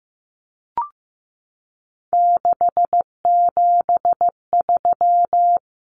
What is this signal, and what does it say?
Morse code, heard as a keyed tone, about 15 wpm, 700 Hz. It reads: 673
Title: Common Cut Numbers - 3-Digit 15wpm